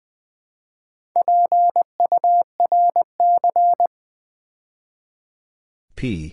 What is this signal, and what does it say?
Morse code, heard as a keyed tone, about 20 wpm, 700 Hz. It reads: PURC
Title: Build A Word - 2 Letters At A Time - 8-Letter Words - 20wpm